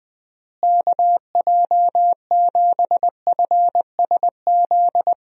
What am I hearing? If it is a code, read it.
Morse code, heard as a keyed tone, about 20 wpm, 700 Hz. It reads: KJ7FSZ
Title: USA Call Signs - 6 Characters - Repeated 2x - 20wpm